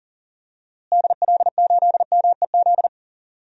Morse code, heard as a keyed tone, about 39 wpm, 700 Hz. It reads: DL8MEZ
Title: DXCC-100 Call Signs - 6 Characters - Repeated 2x - 40wpm